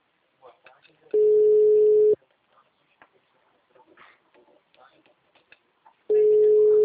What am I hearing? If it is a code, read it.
Ringback tone (Europe, 425 Hz), with speech.